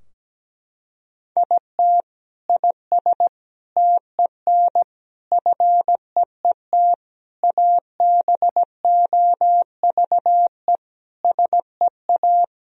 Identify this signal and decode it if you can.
Morse code, heard as a keyed tone, about 17 wpm, 700 Hz. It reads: IT IS TEN FEET ABOVE SEA
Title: Sentences from Top 1000 Words - Review Part 4 - 17wpm